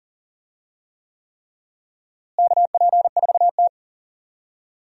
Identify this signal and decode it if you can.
Morse code, heard as a keyed tone, about 40 wpm, 700 Hz. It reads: KP4T
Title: DXCC-100 Call Signs - 4 Characters - Repeated 3x - 40wpm